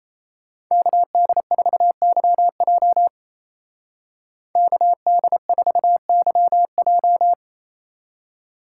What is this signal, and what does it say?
Morse code, heard as a keyed tone, about 37 wpm, 700 Hz. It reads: KD4YJ KD4YJ